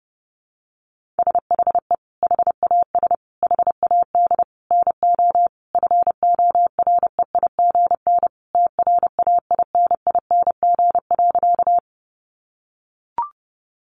Morse code, heard as a keyed tone, about 30 wpm, 700 Hz. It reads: SHE HAS HAD NO FOREIGN TRAINING.
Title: Sentences from Top 1400 Words 30wpm